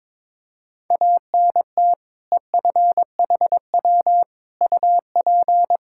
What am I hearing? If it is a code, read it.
Morse code, heard as a keyed tone, about 22 wpm, 700 Hz. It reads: ANT EFHW UP
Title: Practice QSO - Exchange 2 - Info - 22wpm